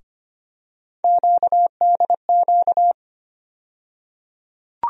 Morse code, heard as a keyed tone, about 25 wpm, 700 Hz. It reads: QDQ
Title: USA Call Sign Postfixes - 3 Letters 25wpm